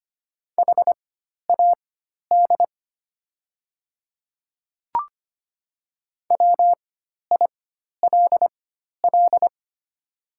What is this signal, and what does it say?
Morse code, heard as a keyed tone, about 25 wpm, spaced as for 12 wpm, 700 Hz. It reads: HAD WILL
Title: 4x Character Spacing - Top 200 Words - 25wpm